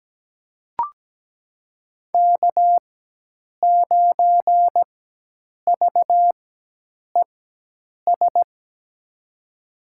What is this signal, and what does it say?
Morse code, heard as a keyed tone, about 17 wpm, spaced as for 8 wpm, 700 Hz. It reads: K9VES
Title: USA Call Signs - 5 Characters - 4x Character Spacing - 17wpm